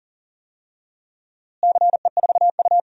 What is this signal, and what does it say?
Morse code, heard as a keyed tone, about 37 wpm, 700 Hz. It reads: CE4U
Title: DXCC-100 Call Signs - 4 Characters - Repeated 2x - 40wpm